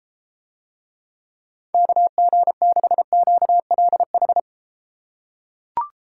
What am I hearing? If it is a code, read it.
Morse code, heard as a keyed tone, about 33 wpm, 700 Hz. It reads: KG6QLH